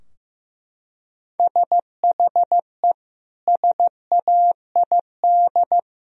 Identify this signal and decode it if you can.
Morse code, heard as a keyed tone, about 15 wpm, 700 Hz. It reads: SHE SAID
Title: Sentences from Top 1000 Words - Review Part 2 - 15wpm